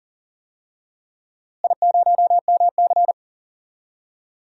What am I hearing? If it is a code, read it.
Morse code, heard as a keyed tone, about 38 wpm, 700 Hz. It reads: I0MC
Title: DXCC-100 Call Signs - 4 Characters - Repeated 3x - 40wpm